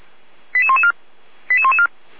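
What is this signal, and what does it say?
Five-tone selective call (ZVEI), heard as quick runs of five tones, tones of about 80 ms. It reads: 88114 88114
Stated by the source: ZVEI five-tone call 88112, received by SDR.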